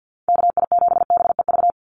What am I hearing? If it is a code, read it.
Morse code, heard as a keyed tone, about 50 wpm, 700 Hz. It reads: KI7BEV
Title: USA Call Signs 50wpm